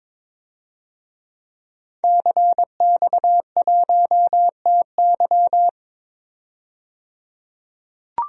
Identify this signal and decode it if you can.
Morse code, heard as a keyed tone, about 21 wpm, 700 Hz. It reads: CX1TY
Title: DXCC-100 Call Signs - 5 Characters - 22wpm